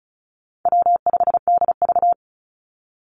Morse code, heard as a keyed tone, about 34 wpm, 700 Hz. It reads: W5DV
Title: Call Signs Repeated 3x - 35wpm